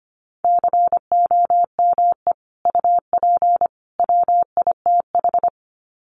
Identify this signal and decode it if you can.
Morse code, heard as a keyed tone, about 25 wpm, 700 Hz. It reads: COME UP WITH
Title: Sets of 3 Words 25wpm